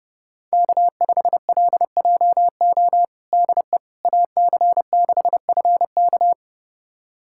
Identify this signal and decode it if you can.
Morse code, heard as a keyed tone, about 30 wpm, 700 Hz. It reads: K5LJO DE AC6FK
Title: USA Callsign DE Callsign - 5 Characters - 30wpm